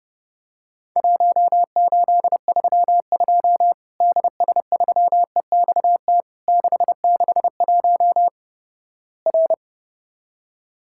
Morse code, heard as a keyed tone, about 32 wpm, 700 Hz. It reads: 1832 DS3EXT 661 R
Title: Worked All Europe DX Contest - QTC Exchange - 30wpm